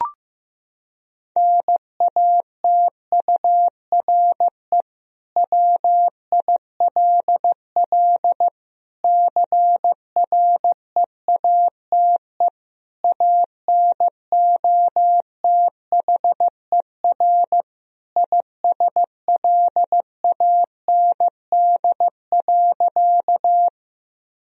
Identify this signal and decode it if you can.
Morse code, heard as a keyed tone, about 16 wpm, 700 Hz. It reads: NATURE WILL CREATE ANOTHER ISLAND.
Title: Sentences from Top 1000 Words - Review Part 3 - 15wpm